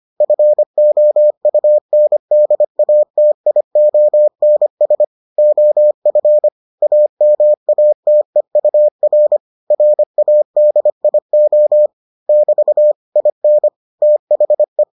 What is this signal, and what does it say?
Morse code, heard as a keyed tone, about 25 wpm, 600 Hz. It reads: FOUNDATIONS OF AMATEUR RADIO <BT> IN THE